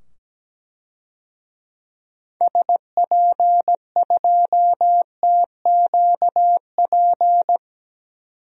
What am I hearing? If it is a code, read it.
Morse code, heard as a keyed tone, about 17 wpm, 700 Hz. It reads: SP2TQP
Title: DXCC-100 Call Signs - 6 Characters - 17wpm